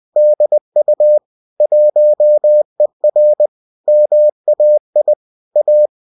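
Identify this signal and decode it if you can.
Morse code, heard as a keyed tone, about 20 wpm, 600 Hz. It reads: DU 1ER MAI A
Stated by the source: UFT bulletin sent in CW.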